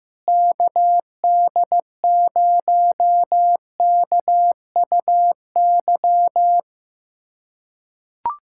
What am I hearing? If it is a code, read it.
Morse code, heard as a keyed tone, about 15 wpm, 700 Hz. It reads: KD0KUY
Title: USA Callsign DE Callsign 15wpm